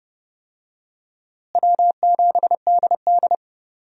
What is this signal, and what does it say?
Morse code, heard as a keyed tone, about 30 wpm, 700 Hz. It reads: W7DD
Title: USA Call Signs - 4 Characters - Repeated 3x - 30wpm